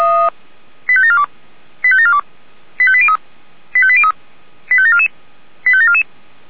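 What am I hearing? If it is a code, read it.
Five-tone selective call (ZVEI), heard as quick runs of five tones, tones of about 70 ms. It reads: 85742 85742 85803 85803 85744 85744
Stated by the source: ZVEI five-tone selective call 85722 received by SDR.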